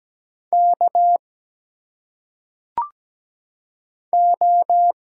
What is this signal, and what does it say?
Morse code, heard as a keyed tone, about 17 wpm, 700 Hz. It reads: K O
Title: Single Letters - Phonetic Pronunciation 17wpm